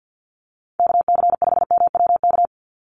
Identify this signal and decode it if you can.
Morse code, heard as a keyed tone, about 50 wpm, 700 Hz. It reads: KC5MWK
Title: USA Call Signs 50wpm